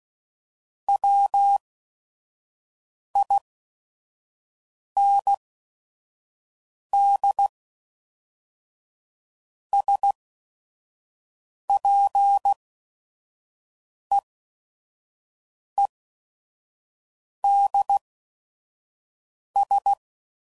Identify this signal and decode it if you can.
Morse code, heard as a keyed tone, about 16 wpm, spaced as for 5 wpm, 800 Hz. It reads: WIND SPEEDS
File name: CW Code Groups @5WPM / Weather Sets / Weather 4 24